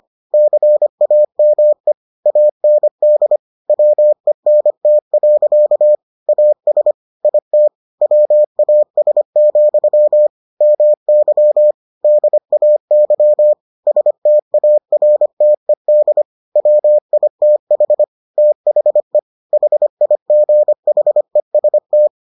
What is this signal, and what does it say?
Morse code, heard as a keyed tone, about 25 wpm, 600 Hz. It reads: CAME AND WENT. AS IT WAS, MY DAY STARTED WITH THE HIGHEST